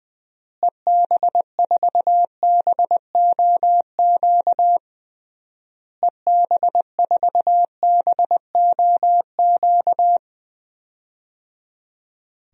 Morse code, heard as a keyed tone, about 20 wpm, 700 Hz. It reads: EB4BOQ EB4BOQ
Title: DXCC-100 Call Signs - 6 Characters - Repeated 3x - 20wpm